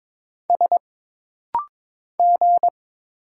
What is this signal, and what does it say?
Morse code, heard as a keyed tone, about 22 wpm, 700 Hz. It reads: S G
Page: This rapid-fire timing is geared to push your ICR (Instant Character Recognition) skills.